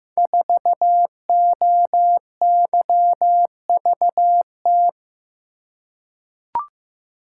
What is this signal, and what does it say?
Morse code, heard as a keyed tone, about 15 wpm, 700 Hz. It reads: 4OYVT
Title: Code Groups - 5 Letter:Numbers - 15wpm